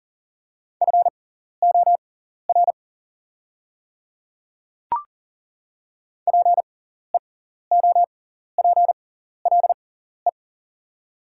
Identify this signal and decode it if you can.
Morse code, heard as a keyed tone, about 40 wpm, spaced as for 14 wpm, 690 Hz. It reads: FOR PEOPLE
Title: 6x Character Spacing - Top 200 Words - 40wpm